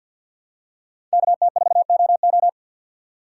Morse code, heard as a keyed tone, about 48 wpm, 700 Hz. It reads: KT4OO
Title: USA Call Signs - 6x Character Spacing - 50wpm